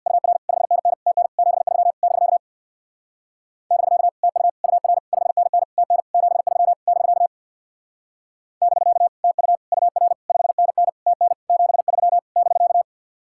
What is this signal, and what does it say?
Morse code, heard as a keyed tone, about 58 wpm, 700 Hz. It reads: UR 5NN TN 73 <BK> <BK> TU UR 5NN TN 73 <BK> <BK> TU UR 5NN TN 73 <BK>
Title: Speed-Racing Common QSO Phrases - 50wpm